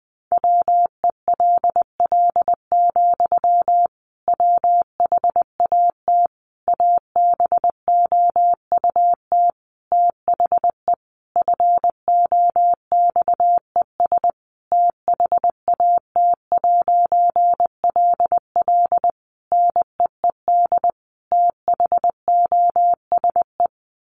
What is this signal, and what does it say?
Morse code, heard as a keyed tone, about 20 wpm, 700 Hz. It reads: WELL, WHAT ABOUT THE FOXES THAT'LL NEED THOSE
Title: A Sound Of Thunder 20wpm